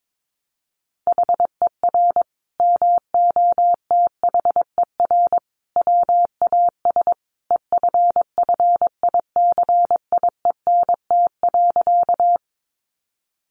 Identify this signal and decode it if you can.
Morse code, heard as a keyed tone, about 22 wpm, 700 Hz. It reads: HER MOTHER WAS EFFICIENT.
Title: Sentences from Top 1300 Words 22wpm